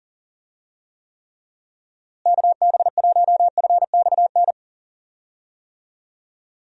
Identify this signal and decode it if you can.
Morse code, heard as a keyed tone, about 37 wpm, 700 Hz. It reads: KB1FXN
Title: USA Call Signs - 6 Characters - Repeated 3x - 40wpm